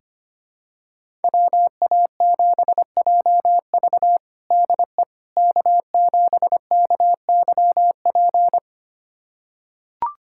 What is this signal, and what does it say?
Morse code, heard as a keyed tone, about 25 wpm, 700 Hz. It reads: WA7JV DE K7KYP
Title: USA Callsign DE Callsign - 5 Characters - 25wpm